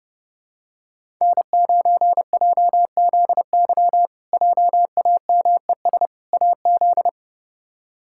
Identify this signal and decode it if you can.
Morse code, heard as a keyed tone, about 32 wpm, 700 Hz. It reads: N9JZY JAMES AZ
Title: K1USN SST QSO Practice - 30wpm